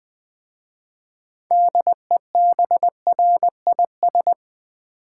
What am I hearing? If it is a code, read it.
Morse code, heard as a keyed tone, about 20 wpm, 700 Hz. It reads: DEBRIS
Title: 6 Letter Words - First 3 Letters Given - 20wpm